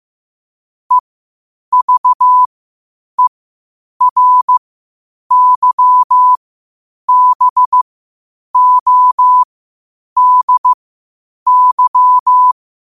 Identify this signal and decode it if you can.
Morse code, heard as a keyed tone, about 15 wpm, spaced as for 8 wpm, 1000 Hz. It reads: EVERYBODY
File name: qotdfiles / 8wpm